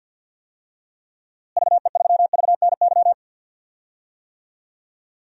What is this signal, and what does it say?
Morse code, heard as a keyed tone, about 51 wpm, 700 Hz. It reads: VE3VNY
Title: DXCC-100 Call Signs - 6 Characters - Repeated 3x - 50wpm